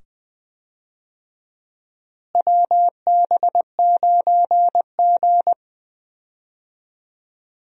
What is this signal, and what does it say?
Morse code, heard as a keyed tone, about 20 wpm, 700 Hz. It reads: WB9G
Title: USA Call Signs - 4 Characters - Repeated 2x - 20wpm